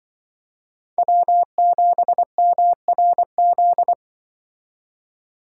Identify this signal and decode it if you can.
Morse code, heard as a keyed tone, about 25 wpm, 700 Hz. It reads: W7MRZ
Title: Speed Racing - USA 1x3 Call Signs 20wpm